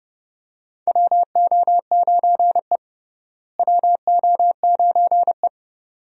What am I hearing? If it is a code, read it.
Morse code, heard as a keyed tone, about 30 wpm, 700 Hz. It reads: WO9E WO9E
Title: USA Call Signs - 4 Characters - Repeated 2x - 30wpm